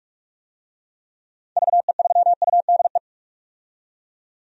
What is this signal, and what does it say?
Morse code, heard as a keyed tone, about 47 wpm, 700 Hz. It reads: VE3UDE
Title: DXCC-100 Call Signs - 6 Characters - 45wpm